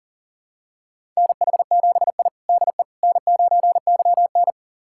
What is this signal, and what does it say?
Morse code, heard as a keyed tone, about 37 wpm, 700 Hz. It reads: NH7I DE N9YN
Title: USA Callsign DE Callsign - 4 Characters - 40wpm